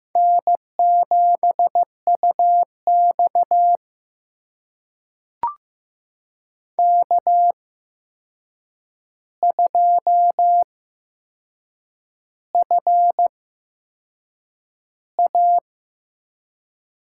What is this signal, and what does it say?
Morse code, heard as a keyed tone, about 15 wpm, spaced as for 6 wpm, 700 Hz. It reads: N7UX K2FA
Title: USA Call Signs - 4 Characters - 8x Character Spacing - 15wpm